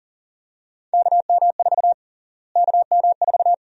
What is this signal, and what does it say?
Morse code, heard as a keyed tone, about 40 wpm, 700 Hz. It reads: KM4 KM4
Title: USA Call Sign Prefixes - Repeated 2x - 40wpm